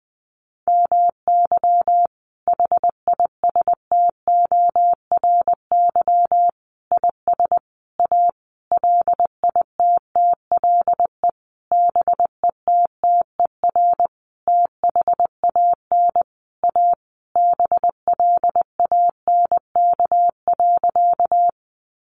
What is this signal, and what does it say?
Morse code, heard as a keyed tone, about 20 wpm, 700 Hz. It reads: MY HISTORY IS A LITTLE BETTER THAN A BLANK.
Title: Sentences from Top 1500 Words 20wpm